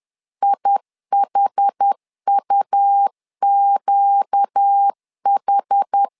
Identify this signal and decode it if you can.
Morse code, heard as a keyed tone, about 10 wpm, 800 Hz. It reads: IHUQH
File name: LEZIONI / 5 LBVC / 5 L 10wpm 2259